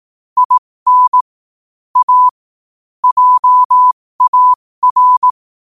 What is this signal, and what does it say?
Morse code, heard as a keyed tone, about 18 wpm, spaced as for 15 wpm, 1000 Hz. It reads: IN A JAR
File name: qotdfiles / 15wpm